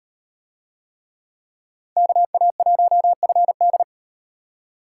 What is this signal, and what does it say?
Morse code, heard as a keyed tone, about 38 wpm, 700 Hz. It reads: KA1FD